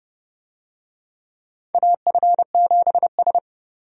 Morse code, heard as a keyed tone, about 29 wpm, 700 Hz. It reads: AF7S
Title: USA Call Signs - 4 Characters - 30wpm